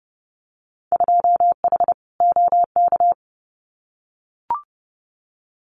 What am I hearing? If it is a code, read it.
Morse code, heard as a keyed tone, about 32 wpm, 700 Hz. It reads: 2H OK
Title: Winter Field Day Exchanges Encore 30wpm